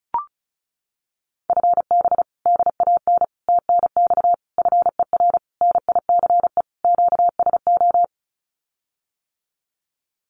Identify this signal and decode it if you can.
Morse code, heard as a keyed tone, about 34 wpm, 700 Hz. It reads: FB DAN TNX FER NICE QSO
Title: Common QSO Phrases Encore 35wpm